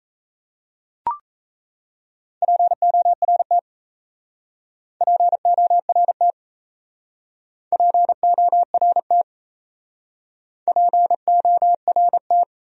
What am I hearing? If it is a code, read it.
Morse code, heard as a keyed tone, about 33 wpm, 700 Hz. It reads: PORT PORT PORT PORT
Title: Speed-Racing Top 200 Words 28wpm